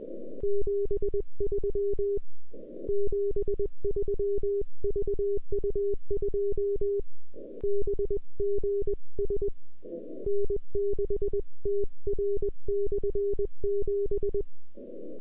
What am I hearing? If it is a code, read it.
Morse code, heard as a keyed tone, about 20 wpm, 400 Hz. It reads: 73 73 VU2BGS N6TR/7